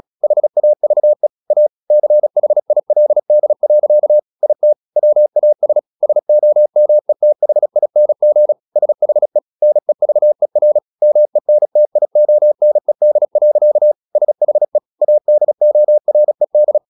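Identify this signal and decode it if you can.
Morse code, heard as a keyed tone, about 36 wpm, 600 Hz. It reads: HAVE A CHILD. IT WAS SOMETHING SHE NEVER MENTIONED. SHE ADORED
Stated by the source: Morse practice at 36 wpm, text of Ken Follett's novel Never, from CW med Gnister.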